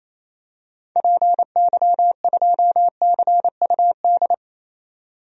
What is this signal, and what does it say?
Morse code, heard as a keyed tone, about 28 wpm, 700 Hz. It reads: PY2CUD
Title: DXCC-100 Call Signs - 6 Characters - 28wpm